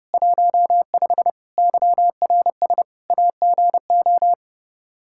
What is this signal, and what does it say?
Morse code, heard as a keyed tone, about 30 wpm, 700 Hz. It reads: 15 YRS AGO
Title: K1USN SST QSO Practice 30wpm